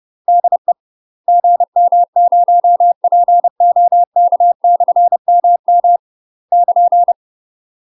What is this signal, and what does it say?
Morse code, heard as a keyed tone, about 30 wpm, 700 Hz. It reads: DE GM0POK/MM <KN>